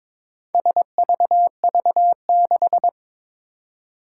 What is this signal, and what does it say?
Morse code, heard as a keyed tone, about 22 wpm, 700 Hz. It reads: SVV6
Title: Four Challenging Letter-Number Combinations 22wpm